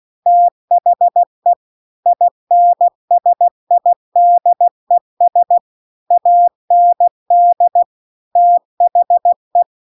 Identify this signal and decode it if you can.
Morse code, heard as a keyed tone, about 16 wpm, 700 Hz. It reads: THE INSIDES AND THE
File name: CW-The-Big-Twoheaded-River-2-16WPM